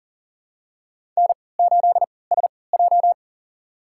Morse code, heard as a keyed tone, about 40 wpm, spaced as for 23 wpm, 700 Hz. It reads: N8SJ
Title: USA Call Signs - 4 Characters - 3x Character Spacing - 40wpm